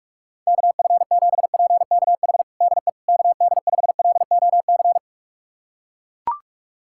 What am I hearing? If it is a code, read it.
Morse code, heard as a keyed tone, about 41 wpm, 690 Hz. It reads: KF7PKH DE KD5LOC
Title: USA Callsign DE Callsign 45wpm